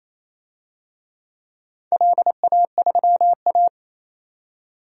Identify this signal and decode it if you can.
Morse code, heard as a keyed tone, about 28 wpm, 700 Hz. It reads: LA3A
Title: DXCC-100 Call Signs - 4 Characters - 28wpm